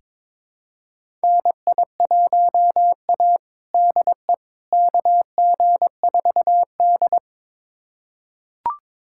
Morse code, heard as a keyed tone, about 21 wpm, 700 Hz. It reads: NI1A DE KG4D
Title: USA Callsign DE Callsign - 4 Characters - 22wpm